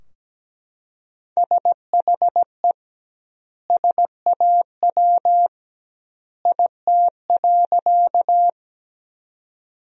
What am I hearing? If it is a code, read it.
Morse code, heard as a keyed tone, about 17 wpm, 700 Hz. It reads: SHE SAW IT.
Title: 2x Word-Spacing - Sentences from Top 300 Words - 17wpm